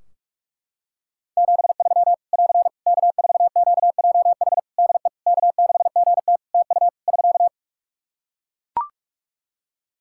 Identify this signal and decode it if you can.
Morse code, heard as a keyed tone, about 44 wpm, 700 Hz. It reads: 73 <AR> K4QJH DE K6GT TU <SK>
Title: Practice QSO - Exchange 4 - Ending - 45wpm